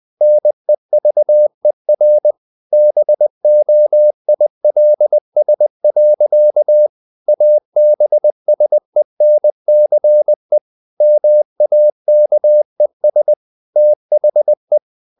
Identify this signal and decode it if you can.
Morse code, heard as a keyed tone, about 20 wpm, 600 Hz. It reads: NEVER BOILS. ABSENCE MAKES THE
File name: CW / Practice / EnglishSayings20